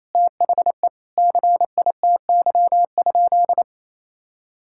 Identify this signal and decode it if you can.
Morse code, heard as a keyed tone, about 28 wpm, 700 Hz. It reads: THE CITY?
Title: Sentences from Top 1600 Words 28wpm